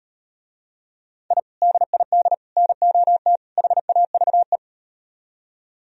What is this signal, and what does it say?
Morse code, heard as a keyed tone, about 38 wpm, 700 Hz. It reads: I DID NOT HAVE